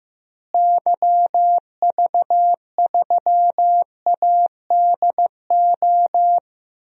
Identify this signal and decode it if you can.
Morse code, heard as a keyed tone, about 15 wpm, 700 Hz. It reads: YV3ADO
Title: DXCC-100 Call Signs - 6 Characters - Repeated 3x - 15wpm